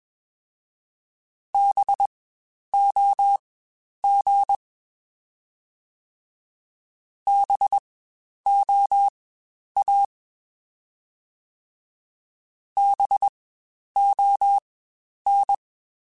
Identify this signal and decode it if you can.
Morse code, heard as a keyed tone, about 21 wpm, spaced as for 10 wpm, 800 Hz. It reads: BOG BOA BON
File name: Spaced Alphas / Spaced Alphas mp3s / Bs sp @20